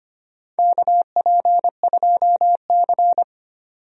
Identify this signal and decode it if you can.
Morse code, heard as a keyed tone, about 25 wpm, 700 Hz. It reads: KP2C
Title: DXCC-100 Call Signs - 4 Characters - 25wpm